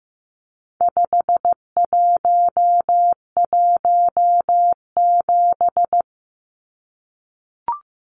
Morse code, heard as a keyed tone, about 15 wpm, 700 Hz. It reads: 5117
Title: Numbers - 4-Digit 15wpm